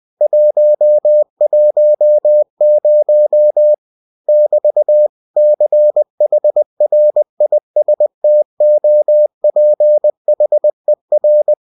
Morse code, heard as a keyed tone, about 20 wpm, 600 Hz. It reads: 110 <BT> CHRISTOPHER
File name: cw-books / winnie-the-pooh / winnie-20wpm-page-0110